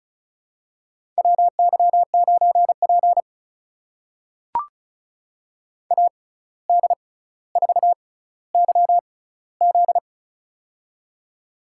Morse code, heard as a keyed tone, about 36 wpm, spaced as for 17 wpm, 700 Hz. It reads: WY9P AD4YZ
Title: USA Call Signs - 6x Character Spacing - 35wpm